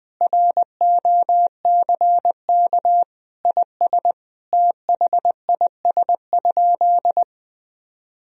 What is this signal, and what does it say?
Morse code, heard as a keyed tone, about 20 wpm, 700 Hz. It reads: ROCK IS THIS?
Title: Sentences from Top 1000 Words - Review Part 3 - 20wpm